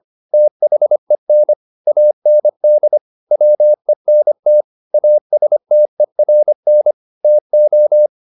Morse code, heard as a keyed tone, about 25 wpm, 600 Hz. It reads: THEN AND WENT ASTERN TO